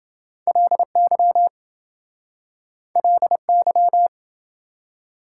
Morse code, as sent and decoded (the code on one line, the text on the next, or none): .-.. -.-- / .-.. -.--
LY LY